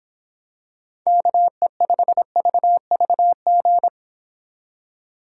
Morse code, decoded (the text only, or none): KE5VVG